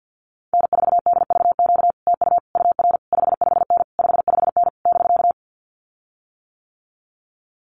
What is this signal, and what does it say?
Morse code, heard as a keyed tone, about 48 wpm, 700 Hz. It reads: N4DVQ TU UR 55N 55N <BK>
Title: Common QSO Phrases Encore 50wpm